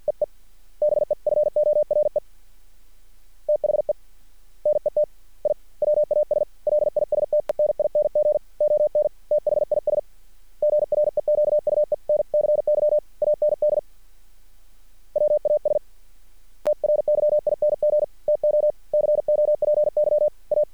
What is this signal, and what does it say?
Morse code, heard as a keyed tone, about 50 wpm, 600 Hz. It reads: EE BEFORE THE NET I WAS LISTENING ON THIS GREQUENCY AND WAS TRYING TO COPY A
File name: archived / CRSnet